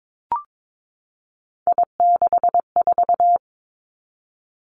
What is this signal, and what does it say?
Morse code, heard as a keyed tone, about 22 wpm, 700 Hz. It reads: I64
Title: Three Challenging Letter-Number Combinations 22wpm